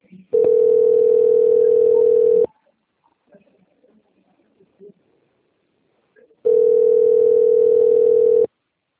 Ringback tone (North America), 440 + 480 Hz.